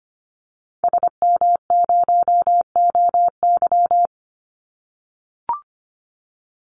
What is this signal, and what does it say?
Morse code, heard as a keyed tone, about 25 wpm, 700 Hz. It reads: SM0OY
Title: Call Signs - 25wpm